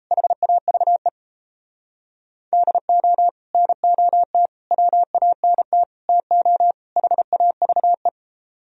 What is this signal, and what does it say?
Morse code, heard as a keyed tone, about 32 wpm, 700 Hz. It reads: HAVE DO NOT WANT TO HAVE
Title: Speed-Racing - Sets of 5 Words 28wpm